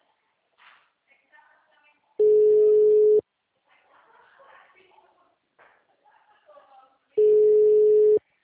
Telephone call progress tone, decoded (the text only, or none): ringback tone